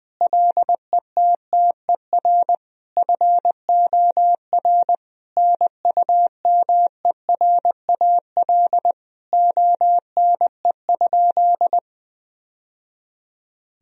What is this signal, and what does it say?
Morse code, heard as a keyed tone, about 20 wpm, 700 Hz. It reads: LETTER FOR NUMERAL ONE?
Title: Sentences from Top 1000 Words - Review Part 2 - 20wpm